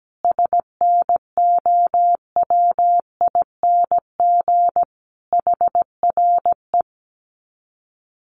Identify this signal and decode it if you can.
Morse code, heard as a keyed tone, about 17 wpm, 700 Hz. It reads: SNOWING HRE
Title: Common QSO Phrases 17wpm